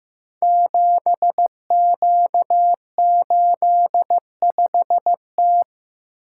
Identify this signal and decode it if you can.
Morse code, heard as a keyed tone, about 15 wpm, 700 Hz. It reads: 7Q85T
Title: Code Groups - 5 Letter:Numbers - 15wpm